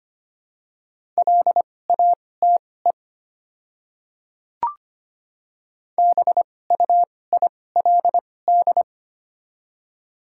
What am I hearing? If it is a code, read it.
Morse code, heard as a keyed tone, about 25 wpm, spaced as for 18 wpm, 700 Hz. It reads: LATE BUILD
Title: 2x Character Spacing - Top 500 Words - 25wpm